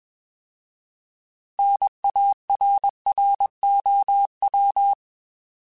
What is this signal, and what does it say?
Morse code, heard as a keyed tone, about 21 wpm, 800 Hz. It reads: NARROW
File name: Alphabetical / Alphabetical mp3s / Ns @20